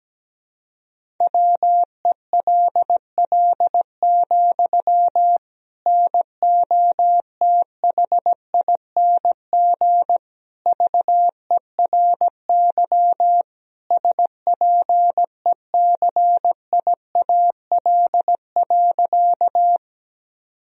Morse code, heard as a keyed tone, about 17 wpm, 700 Hz. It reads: WELL, NOTHING VERY SPECIAL.
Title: Sentences from Top 1000 Words - Review Part 3 - 17wpm